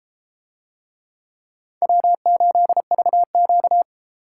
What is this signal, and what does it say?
Morse code, heard as a keyed tone, about 31 wpm, 700 Hz. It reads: W8VQ